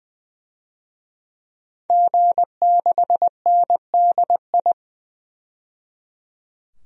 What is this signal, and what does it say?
Morse code, heard as a keyed tone, about 20 wpm, 700 Hz. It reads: G6NDI
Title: DXCC-100 Call Signs - 5 Characters - 20wpm